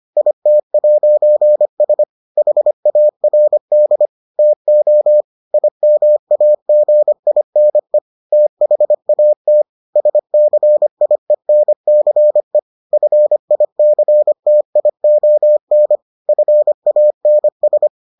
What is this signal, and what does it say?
Morse code, heard as a keyed tone, about 25 wpm, 600 Hz. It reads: IT'S HARD TO IMAGINE THAT SCIENCE FICTION FANS